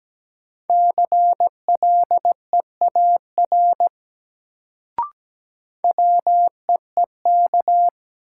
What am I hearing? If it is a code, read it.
Morse code, heard as a keyed tone, about 17 wpm, 700 Hz. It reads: CLEAR WEEK
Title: Sentences from Top 500 Words - New Word Drill - 17wpm